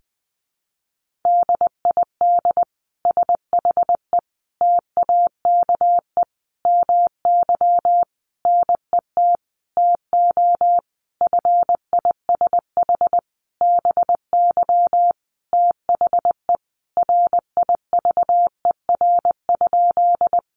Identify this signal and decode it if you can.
Morse code, heard as a keyed tone, about 20 wpm, 700 Hz. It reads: DID SHE TAKE MY NET TO FISH BY THE RIVER?
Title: Sentences from Top 1400 Words 20wpm